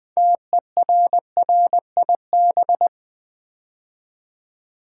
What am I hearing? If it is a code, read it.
Morse code, heard as a keyed tone, about 20 wpm, 700 Hz. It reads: TERRIB
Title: Build A Word - 3 Letters At A Time - 8-Letter Words - 20wpm